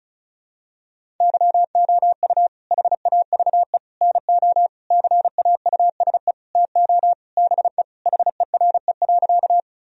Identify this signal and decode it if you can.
Morse code, heard as a keyed tone, about 36 wpm, 700 Hz. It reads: YOU HAVE NO CAUSE TO BE HERE.
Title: Sentences from Top 1000 Words - Review Part 1 - 35wpm